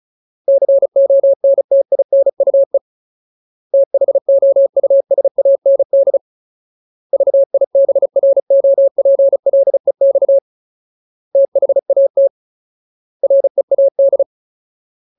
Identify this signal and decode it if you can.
Morse code, heard as a keyed tone, about 34 wpm, 550 Hz. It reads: CONTINUE THOUSAND VIBROPLEX THAT READ
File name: Ultimate Word List / 35WPM Ultimate